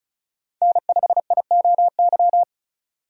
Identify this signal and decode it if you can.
Morse code, heard as a keyed tone, about 36 wpm, 700 Hz. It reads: N5IOY
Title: USA Call Signs - 5 Characters - 35wpm